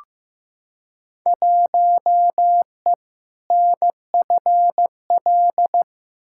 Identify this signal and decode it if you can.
Morse code, heard as a keyed tone, about 15 wpm, 700 Hz. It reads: 1E NFL
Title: Field Day Exchanges - 15wpm